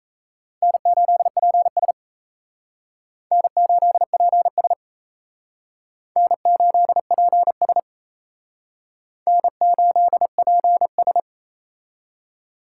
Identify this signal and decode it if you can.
Morse code, heard as a keyed tone, about 34 wpm, 700 Hz. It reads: N8PS N8PS N8PS N8PS
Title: Speed Racing - USA 1x2 Call Signs 28wpm